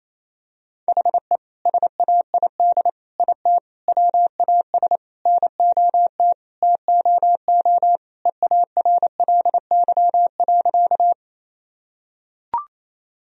Morse code, heard as a keyed tone, about 28 wpm, 700 Hz. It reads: HE SAID IT WAS NOT TOO EARLY.
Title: Sentences from Top 1000 Words - Review Part 2 - 28wpm